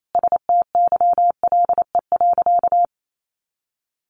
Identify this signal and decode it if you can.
Morse code, heard as a keyed tone, about 28 wpm, 700 Hz. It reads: STYLE.
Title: Sentences from Top 800 Words 28wpm